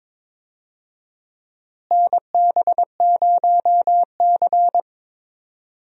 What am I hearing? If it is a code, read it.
Morse code, heard as a keyed tone, about 22 wpm, 700 Hz. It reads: NB0C
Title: USA Call Signs - 4 Characters - Repeated 3x - 22wpm